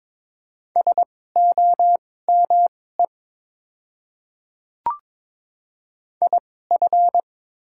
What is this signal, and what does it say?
Morse code, heard as a keyed tone, about 22 wpm, spaced as for 16 wpm, 700 Hz. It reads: SOME IF
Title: 2x Character Spacing - Top 200 Words - 22wpm